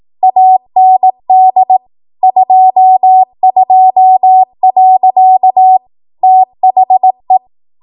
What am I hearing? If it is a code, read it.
Morse code, heard as a keyed tone, about 18 wpm, 750 Hz. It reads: AND 22. THE